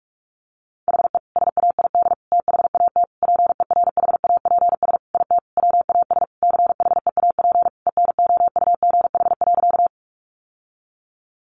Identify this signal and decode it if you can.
Morse code, heard as a keyed tone, about 44 wpm, 700 Hz. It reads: HE SAID THAT PERHAPS IT WAS CHEAP ENOUGH.
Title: Sentences from Top 1200 Words 45wpm